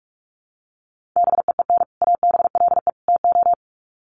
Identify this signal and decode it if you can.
Morse code, heard as a keyed tone, about 44 wpm, 700 Hz. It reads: BEEN ABLE TO
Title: Sets of 3 Words 45wpm